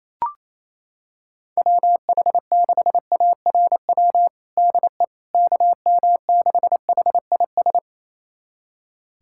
Morse code, heard as a keyed tone, about 28 wpm, 700 Hz. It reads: WH6ARW DE KM6HIS
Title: USA Callsign DE Callsign - 6 Characters - 28wpm